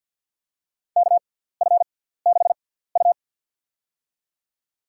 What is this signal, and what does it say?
Morse code, heard as a keyed tone, about 52 wpm, spaced as for 17 wpm, 700 Hz. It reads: KF6U
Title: USA Call Signs - 4 Characters - 6x Character Spacing - 50wpm